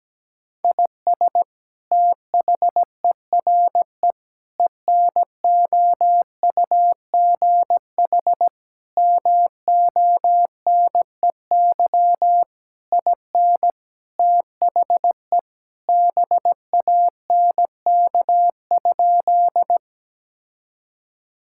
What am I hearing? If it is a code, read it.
Morse code, heard as a keyed tone, about 17 wpm, 700 Hz. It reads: IS THERE ENOUGH MONEY IN THE BANK?
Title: Sentences from Top 1000 Words - Review Part 6 - 17wpm